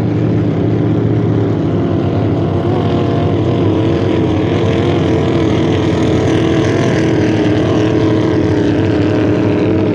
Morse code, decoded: none